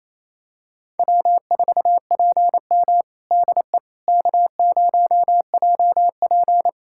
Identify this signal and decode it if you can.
Morse code, heard as a keyed tone, about 27 wpm, 700 Hz. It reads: W4PM DE K0JP
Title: Practice QSO - Exchange 4 - Ending - 28wpm